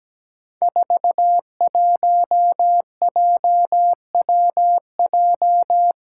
Morse code, heard as a keyed tone, about 17 wpm, 700 Hz. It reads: 41JWJ
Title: Five Challenging Letter-Number Combinations 17wpm